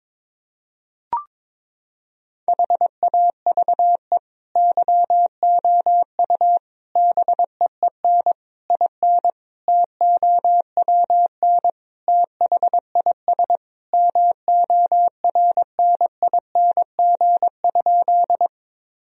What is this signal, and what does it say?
Morse code, heard as a keyed tone, about 22 wpm, 700 Hz. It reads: HAVE YOU BEEN IN TOWN THIS MORNING?
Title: Sentences from Top 1000 Words - Review Part 2 - 22wpm